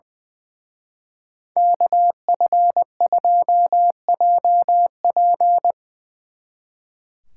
Morse code, heard as a keyed tone, about 22 wpm, 700 Hz. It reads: KF2JP
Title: Speed Racing - USA 2x2 Call Signs 20wpm